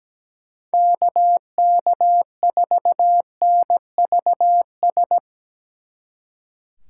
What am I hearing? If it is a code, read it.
Morse code, heard as a keyed tone, about 19 wpm, 700 Hz. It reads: KK4NVS
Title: Speed Racing - USA Call Signs 17wpm